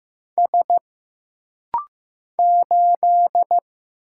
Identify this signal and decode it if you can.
Morse code, heard as a keyed tone, about 15 wpm, 700 Hz. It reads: S 8
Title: Single Letter-Number Alt - Rapid-Fire 15wpm